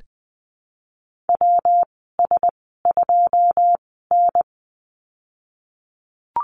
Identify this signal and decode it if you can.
Morse code, heard as a keyed tone, about 20 wpm, spaced as for 14 wpm, 700 Hz. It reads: WS2N
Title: 161 Lesson 26g - Letter g - Call Signs 20wpm